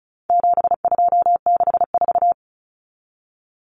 Morse code, heard as a keyed tone, about 34 wpm, 700 Hz. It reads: 7264
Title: Numbers - 4-Digit 35wpm